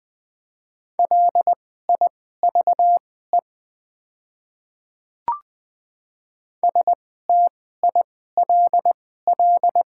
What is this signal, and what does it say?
Morse code, heard as a keyed tone, about 20 wpm, spaced as for 15 wpm, 700 Hz. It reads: LIVE STILL
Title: 2x Character Spacing - Top 500 Words - 20wpm